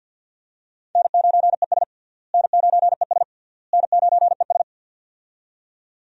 Morse code, heard as a keyed tone, about 47 wpm, 700 Hz. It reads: N9ES N9ES N9ES
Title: USA Call Signs - 4 Characters - Repeated 3x - 50wpm